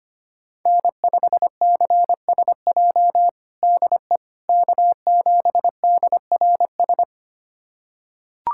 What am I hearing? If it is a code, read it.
Morse code, heard as a keyed tone, about 25 wpm, 700 Hz. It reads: N5CSJ DE K7DRS
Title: USA Callsign DE Callsign - 5 Characters - 25wpm